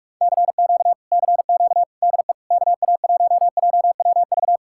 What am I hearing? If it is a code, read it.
Morse code, heard as a keyed tone, about 40 wpm, 700 Hz. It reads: CQ CQ DE KA1JWV